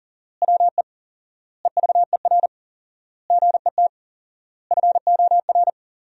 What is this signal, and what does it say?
Morse code, heard as a keyed tone, about 40 wpm, 700 Hz. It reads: WE EVER GET FOR
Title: 4x Word-Spacing - Sentences from Top 400 Words - 40wpm